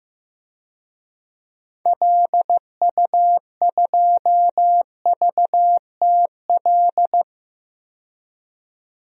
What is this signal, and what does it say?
Morse code, heard as a keyed tone, about 15 wpm, 700 Hz. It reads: LU2VTL
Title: DXCC-100 Call Signs - 6 Characters - 15wpm